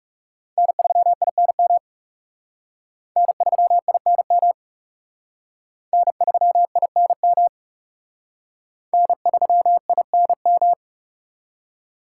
Morse code, heard as a keyed tone, about 37 wpm, 700 Hz. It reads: N3INM N3INM N3INM N3INM
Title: Speed Racing - USA 1x3 Call Signs 30wpm